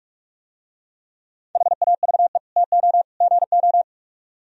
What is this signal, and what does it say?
Morse code, heard as a keyed tone, about 44 wpm, 700 Hz. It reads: HAVE TO GO